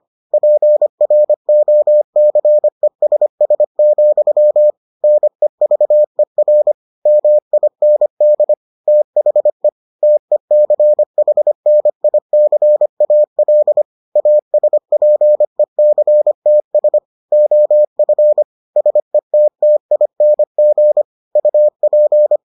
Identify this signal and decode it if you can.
Morse code, heard as a keyed tone, about 25 wpm, 600 Hz. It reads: PROCESS, NEVER MIND THE TECHNICAL ASPECTS OF SETTING UP